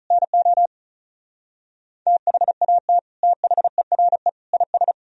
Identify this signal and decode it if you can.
Morse code, heard as a keyed tone, about 46 wpm, 700 Hz. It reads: NO THAT THERE IS